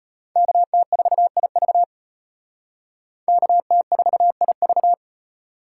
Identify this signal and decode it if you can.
Morse code, heard as a keyed tone, about 32 wpm, 700 Hz. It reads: KT4IV KT4IV